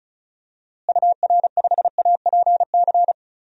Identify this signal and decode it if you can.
Morse code, heard as a keyed tone, about 36 wpm, 700 Hz. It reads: UR5APC
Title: DXCC-100 Call Signs - 6 Characters - Repeated 2x - 35wpm